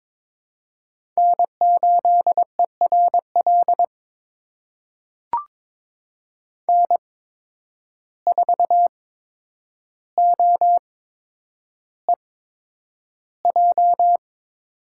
Morse code, heard as a keyed tone, about 22 wpm, spaced as for 9 wpm, 700 Hz. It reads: N8ERL N4OEJ
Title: USA Call Signs - 5 Characters - 8x Character Spacing - 22wpm